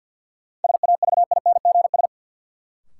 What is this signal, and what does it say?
Morse code, heard as a keyed tone, about 53 wpm, 700 Hz. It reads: SAVINGS
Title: Sentences from Top 1200 Words - New Word Drill Alt - 50wpm